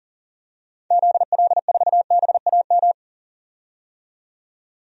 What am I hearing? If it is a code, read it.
Morse code, heard as a keyed tone, about 40 wpm, 700 Hz. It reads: ZL4BAM
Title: DXCC-100 Call Signs - 6 Characters - 40wpm